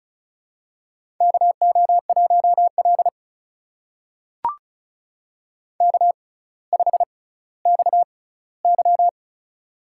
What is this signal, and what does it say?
Morse code, heard as a keyed tone, about 36 wpm, spaced as for 18 wpm, 700 Hz. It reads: KO1L K5XY
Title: USA Call Signs - 4 Characters - 6x Character Spacing - 35wpm